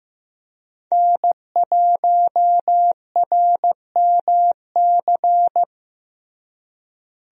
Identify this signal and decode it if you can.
Morse code, heard as a keyed tone, about 15 wpm, 700 Hz. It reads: N1RMC
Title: USA Call Signs - 5 Characters - Repeated 3x - 15wpm